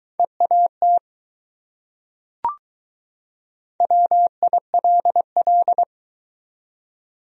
Morse code, heard as a keyed tone, about 21 wpm, 700 Hz. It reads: EAT WILL